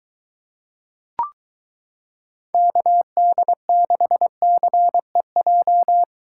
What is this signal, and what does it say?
Morse code, heard as a keyed tone, about 22 wpm, 700 Hz. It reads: KD6CEJ